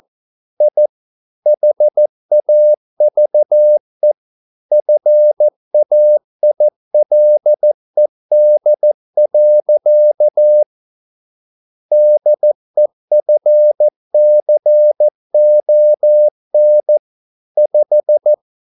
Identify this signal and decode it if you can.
Morse code, heard as a keyed tone, about 14 wpm, 600 Hz. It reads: I HAVE FAILED. DEFCON 5